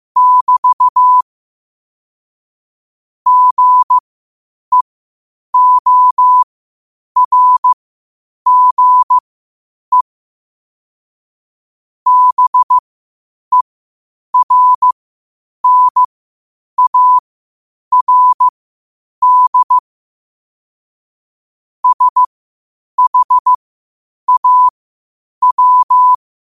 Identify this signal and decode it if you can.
Morse code, heard as a keyed tone, about 15 wpm, spaced as for 8 wpm, 1000 Hz. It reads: <BT> GEORGE BERNARD SHAW
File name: qotdfiles / 8wpm